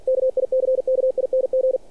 Morse code, heard as a keyed tone, about 48 wpm, 520 Hz. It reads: KICKING
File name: archived / CRSnet